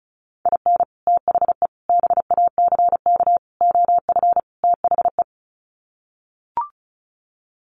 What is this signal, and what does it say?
Morse code, heard as a keyed tone, about 34 wpm, 700 Hz. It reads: IN THE BACK OF THE
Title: Sets of 5 Words 35wpm